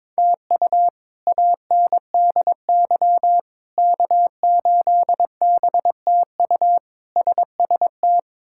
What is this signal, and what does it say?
Morse code, heard as a keyed tone, about 22 wpm, 700 Hz. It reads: TU ANDY K8BTU SST
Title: K1USN SST QSO Practice - 22wpm